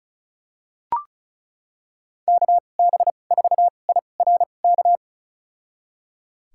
Morse code, heard as a keyed tone, about 34 wpm, spaced as for 25 wpm, 700 Hz. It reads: KB4IRK
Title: USA Call Signs - 6 Characters - 2x Character Spacing - 35wpm